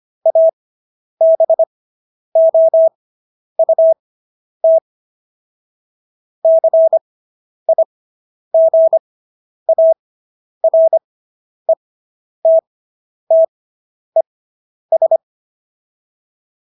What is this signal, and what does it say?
Morse code, heard as a keyed tone, about 25 wpm, spaced as for 10 wpm, 650 Hz. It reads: ABOUT CIGARETTES